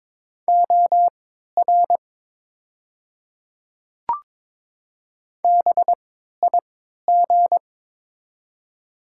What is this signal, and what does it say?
Morse code, heard as a keyed tone, about 22 wpm, spaced as for 13 wpm, 690 Hz. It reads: OR BIG
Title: 3x Character Spacing - Top 300 Words - 22wpm